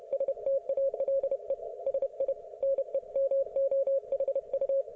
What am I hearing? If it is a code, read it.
Morse code, heard as a keyed tone, about 31 wpm, 530 Hz. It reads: STALE SI NEMOHU